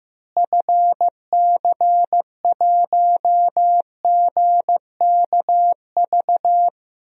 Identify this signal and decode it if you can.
Morse code, heard as a keyed tone, about 15 wpm, 700 Hz. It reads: FC1GKV
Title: DXCC-100 Call Signs - 6 Characters - Repeated 2x - 15wpm